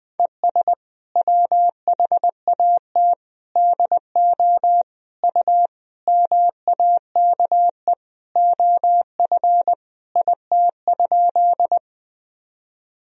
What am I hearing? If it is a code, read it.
Morse code, heard as a keyed tone, about 20 wpm, 700 Hz. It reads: ES WHAT DO U MAKE OF IT?
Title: Sentences from Top 100 Words and Abbreviations 20wpm